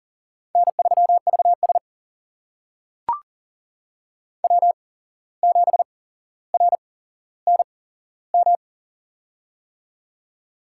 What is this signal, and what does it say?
Morse code, heard as a keyed tone, about 40 wpm, spaced as for 16 wpm, 700 Hz. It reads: N3VS W7RNM
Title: USA Call Signs - 8x Character Spacing - 40wpm